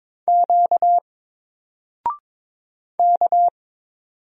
Morse code, heard as a keyed tone, about 22 wpm, 700 Hz. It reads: Q K
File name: Single-Letter-Number-22wpm